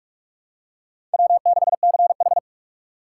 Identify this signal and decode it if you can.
Morse code, heard as a keyed tone, about 45 wpm, 700 Hz. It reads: W6CH
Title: USA Call Signs - 4 Characters - 4x Character Spacing - 45wpm